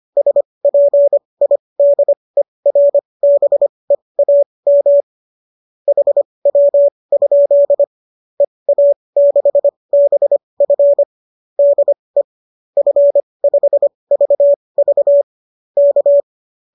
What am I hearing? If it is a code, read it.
Morse code, heard as a keyed tone, about 25 wpm, spaced as for 20 wpm, 570 Hz. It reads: SPIDERBEAM HW? EA6BF DE F5VV K